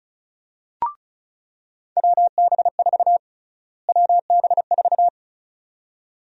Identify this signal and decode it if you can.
Morse code, heard as a keyed tone, about 34 wpm, 700 Hz. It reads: WB4 WB4
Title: USA Call Sign Prefixes - Repeated 2x - 35wpm